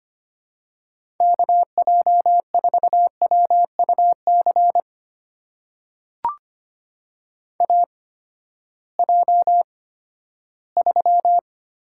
Morse code, heard as a keyed tone, about 25 wpm, spaced as for 14 wpm, 700 Hz. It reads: KJ4WUC AJ3
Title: USA Call Signs - 8x Character Spacing - 25wpm